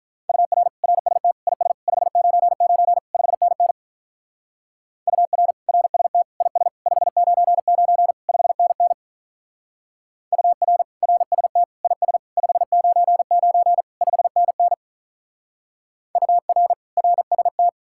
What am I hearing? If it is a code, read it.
Morse code, heard as a keyed tone, about 44 wpm, 700 Hz. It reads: UR RST IS 599 5NN UR RST IS 599 5NN UR RST IS 599 5NN UR RST